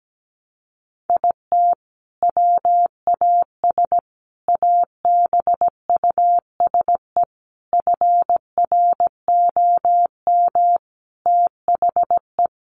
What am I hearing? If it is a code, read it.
Morse code, heard as a keyed tone, about 17 wpm, 700 Hz. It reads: IT WAS ABUSE FROM THE
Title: Sentences from Top 1300 Words 17wpm